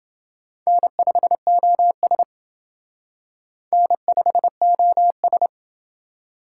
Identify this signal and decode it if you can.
Morse code, heard as a keyed tone, about 28 wpm, 700 Hz. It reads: N5OS N5OS